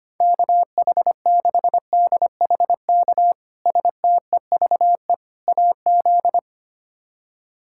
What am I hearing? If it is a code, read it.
Morse code, heard as a keyed tone, about 25 wpm, 700 Hz. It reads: KH6DHK STEVE AZ
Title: K1USN SST QSO Practice - 25wpm